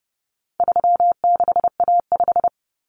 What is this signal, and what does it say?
Morse code, heard as a keyed tone, about 30 wpm, 700 Hz. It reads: 36A5
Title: Common Cut Numbers - 4-Digit 30wpm